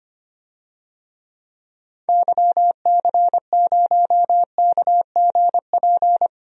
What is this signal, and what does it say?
Morse code, heard as a keyed tone, about 25 wpm, 700 Hz. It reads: YC0KGP
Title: DXCC-100 Call Signs - 6 Characters - 25wpm